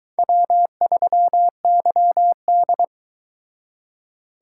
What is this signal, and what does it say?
Morse code, heard as a keyed tone, about 22 wpm, 700 Hz. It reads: W3YD